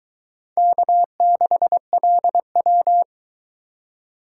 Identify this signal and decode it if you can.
Morse code, heard as a keyed tone, about 22 wpm, 700 Hz. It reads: K6LW